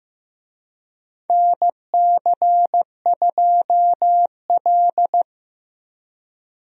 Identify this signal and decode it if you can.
Morse code, heard as a keyed tone, about 15 wpm, 700 Hz. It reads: NC2L
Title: USA Call Signs - 4 Characters - 15wpm